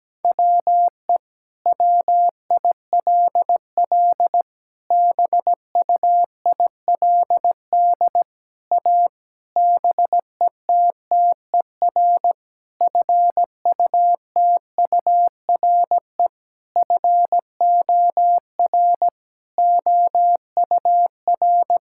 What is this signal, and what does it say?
Morse code, heard as a keyed tone, about 17 wpm, 700 Hz. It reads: WE WILL BUILD A BETTER FUTURE FOR OUR
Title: Sentences from Top 1000 Words - Review Part 6 - 17wpm